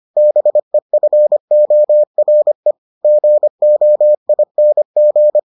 Morse code, heard as a keyed tone, about 25 wpm, 600 Hz. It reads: BEFORE GOING